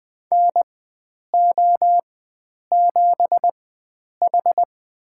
Morse code, heard as a keyed tone, about 20 wpm, spaced as for 9 wpm, 700 Hz. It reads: NO7H
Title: USA Call Signs - 4 Characters - 4x Character Spacing - 20wpm